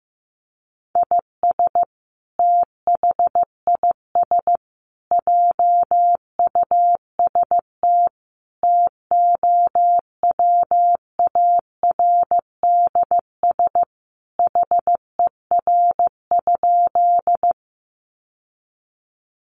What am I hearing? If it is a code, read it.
Morse code, heard as a keyed tone, about 15 wpm, 700 Hz. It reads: IS THIS JUST TOWARDS HER?
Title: Sentences from Top 1100 Words 15wpm